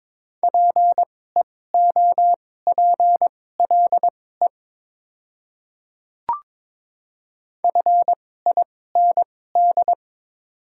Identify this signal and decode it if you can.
Morse code, heard as a keyed tone, about 22 wpm, spaced as for 16 wpm, 700 Hz. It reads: PEOPLE FIND
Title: 2x Character Spacing - Top 100 Words - 22wpm